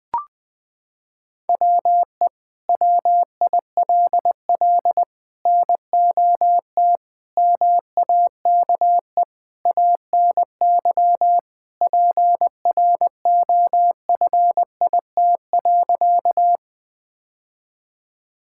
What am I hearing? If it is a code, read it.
Morse code, heard as a keyed tone, about 20 wpm, 700 Hz. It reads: WE WILL NOT MAKE ANY PROFIT.
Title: Sentences from Top 1000 Words - Review Part 5 - 20wpm